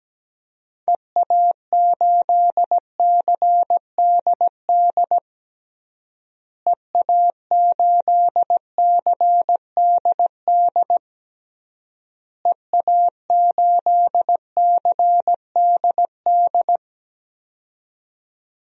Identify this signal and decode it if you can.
Morse code, heard as a keyed tone, about 17 wpm, 700 Hz. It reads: EA8CDD EA8CDD EA8CDD
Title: DXCC-100 Call Signs - 6 Characters - Repeated 3x - 17wpm